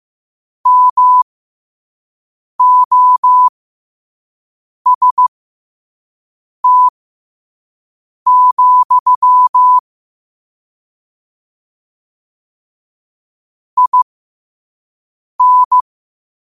Morse code, heard as a keyed tone, about 15 wpm, spaced as for 5 wpm, 1000 Hz. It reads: MOST, IN
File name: qotdfiles / 5wpm